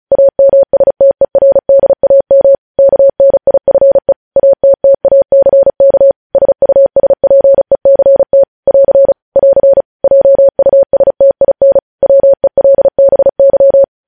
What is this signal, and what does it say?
Morse code, heard as a keyed tone, about 36 wpm, 550 Hz. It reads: AMSTERDAM KNIFE ATTACK SUSPECT <AR> <AR> JUSTIN WELBY
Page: News headlines in Morse code at 35 wpm